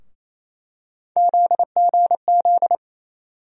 Morse code, heard as a keyed tone, about 28 wpm, 700 Hz. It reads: ZGZ
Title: USA Call Sign Postfixes - 3 Letters 28wpm